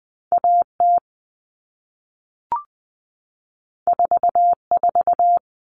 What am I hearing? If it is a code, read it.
Morse code, heard as a keyed tone, about 20 wpm, 700 Hz. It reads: AT 44
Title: All Cut Numbers - 2-Digit 20wpm